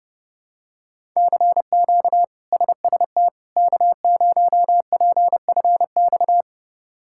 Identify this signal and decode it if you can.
Morse code, heard as a keyed tone, about 31 wpm, 700 Hz. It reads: CQ SST K0PFX
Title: K1USN SST QSO Practice - 30wpm